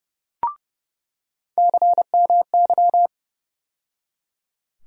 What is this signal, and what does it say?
Morse code, heard as a keyed tone, about 30 wpm, 700 Hz. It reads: CMY
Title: USA Call Sign Postfixes - 3 Letters 30wpm